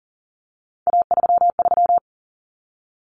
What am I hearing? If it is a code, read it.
Morse code, heard as a keyed tone, about 40 wpm, 700 Hz. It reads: A33
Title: Common Cut Numbers - 3-Digit 40wpm